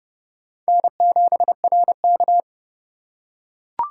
Morse code, heard as a keyed tone, about 29 wpm, 700 Hz. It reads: N7RK